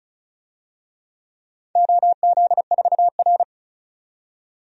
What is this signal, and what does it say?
Morse code, heard as a keyed tone, about 34 wpm, 700 Hz. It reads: OZ4R
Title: DXCC-100 Call Signs - 4 Characters - 35wpm